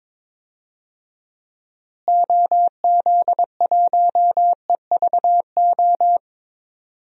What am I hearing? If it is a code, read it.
Morse code, heard as a keyed tone, about 22 wpm, 700 Hz. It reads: OZ1EVO
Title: DXCC-100 Call Signs - 6 Characters - Repeated 2x - 22wpm